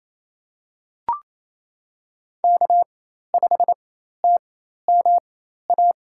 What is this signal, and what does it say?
Morse code, heard as a keyed tone, about 28 wpm, spaced as for 13 wpm, 700 Hz. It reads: K5TMA
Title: USA Call Signs - 5 Characters - 4x Character Spacing - 28wpm